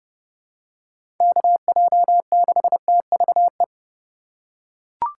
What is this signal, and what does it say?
Morse code, heard as a keyed tone, about 29 wpm, 700 Hz. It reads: KJ6TVE
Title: USA Call Signs - 6 Characters - 4x Character Spacing - 30wpm